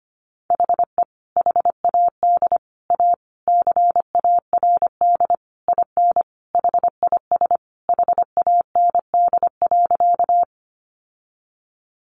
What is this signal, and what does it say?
Morse code, heard as a keyed tone, about 25 wpm, 700 Hz. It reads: HE HAD A CARD IN HIS HAND.
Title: Sentences from Top 600 Words 25wpm